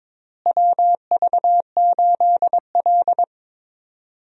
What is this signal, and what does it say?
Morse code, heard as a keyed tone, about 22 wpm, 700 Hz. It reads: WV8L
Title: USA Call Signs - 4 Characters - Repeated 2x - 22wpm